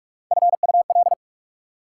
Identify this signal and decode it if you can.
Morse code, heard as a keyed tone, about 45 wpm, 700 Hz. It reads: FUL
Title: Prefixes and Suffixes 45wpm